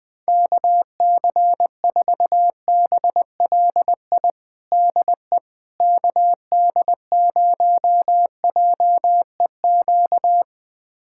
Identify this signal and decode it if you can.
Morse code, heard as a keyed tone, about 21 wpm, 700 Hz. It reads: KC4BLI DE KD0JEQ
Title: USA Callsign DE Callsign 20wpm